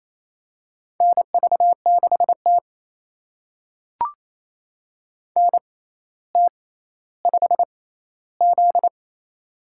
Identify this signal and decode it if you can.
Morse code, heard as a keyed tone, about 28 wpm, spaced as for 14 wpm, 700 Hz. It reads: NV6T NT5Z
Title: USA Call Signs - 4 Characters - 6x Character Spacing - 28wpm